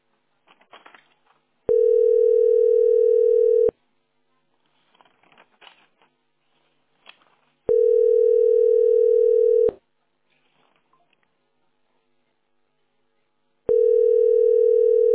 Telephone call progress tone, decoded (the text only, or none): ringback tone